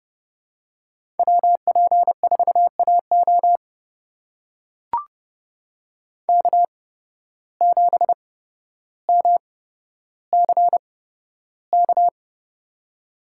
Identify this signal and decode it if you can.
Morse code, heard as a keyed tone, about 30 wpm, spaced as for 13 wpm, 700 Hz. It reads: WP4AO K7MCK
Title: USA Call Signs - 5 Characters - 8x Character Spacing - 30wpm